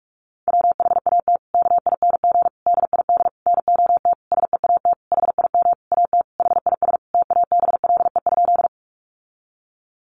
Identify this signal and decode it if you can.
Morse code, heard as a keyed tone, about 44 wpm, 700 Hz. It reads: WHAT KING DID NOT SEAT HIM AT HIS TABLE?
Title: Sentences from Top 1500 Words 45wpm